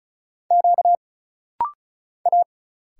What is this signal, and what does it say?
Morse code, heard as a keyed tone, about 35 wpm, 700 Hz. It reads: Q A
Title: Single Letter-Number Alt - Rapid-Fire 35wpm